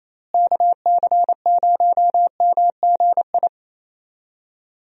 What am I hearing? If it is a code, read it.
Morse code, heard as a keyed tone, about 28 wpm, 700 Hz. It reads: KC0MGI
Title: USA Call Signs - 6 Characters - Repeated 3x - 28wpm